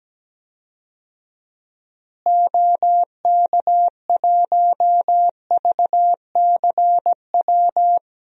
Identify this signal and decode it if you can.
Morse code, heard as a keyed tone, about 17 wpm, 700 Hz. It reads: OK1VCW
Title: DXCC-100 Call Signs - 6 Characters - Repeated 3x - 17wpm